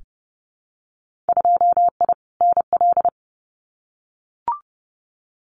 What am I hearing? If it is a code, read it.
Morse code, heard as a keyed tone, about 30 wpm, 700 Hz. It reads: 2I NL
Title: Winter Field Day Exchanges Encore 30wpm